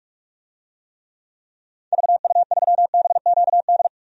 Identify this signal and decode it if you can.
Morse code, heard as a keyed tone, about 46 wpm, 700 Hz. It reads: VU3BQD
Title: DXCC-100 Call Signs - 6 Characters - 45wpm